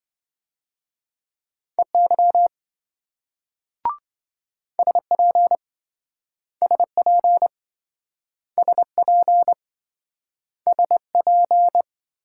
Morse code, heard as a keyed tone, about 26 wpm, 700 Hz. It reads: EY SP SP SP SP